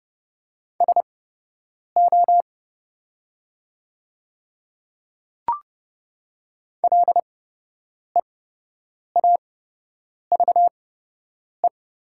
Morse code, heard as a keyed tone, about 30 wpm, spaced as for 8 wpm, 700 Hz. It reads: SO LEAVE
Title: 8x Character Spacing - Top 500 Words - 30wpm